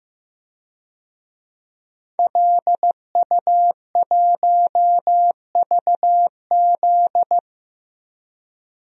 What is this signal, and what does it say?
Morse code, heard as a keyed tone, about 15 wpm, 700 Hz. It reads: LU1VZ
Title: DXCC-100 Call Signs - 5 Characters - 15wpm